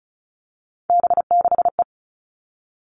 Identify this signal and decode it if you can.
Morse code, heard as a keyed tone, about 34 wpm, 700 Hz. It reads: B6E
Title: All Cut Numbers - 3-Digit 35wpm